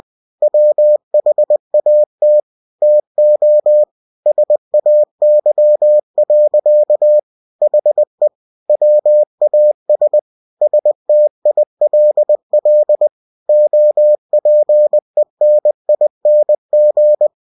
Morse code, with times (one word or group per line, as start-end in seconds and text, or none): 0.4-2.4 WHAT
2.8-3.8 TO
4.3-7.2 SAY.
7.6-8.3 HE
8.7-10.2 WAS
10.6-13.1 STILL
13.5-17.3 OPENING